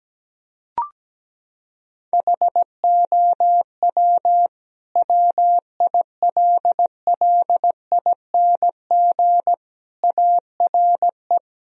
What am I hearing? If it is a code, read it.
Morse code, heard as a keyed tone, about 17 wpm, 700 Hz. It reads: HOW WILLING ARE
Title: Sentences from Top 1000 Words - Review Part 3 - 17wpm